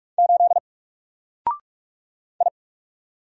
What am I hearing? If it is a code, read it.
Morse code, heard as a keyed tone, about 44 wpm, 700 Hz. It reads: 8 I
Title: Single Letter-Number Alt 45wpm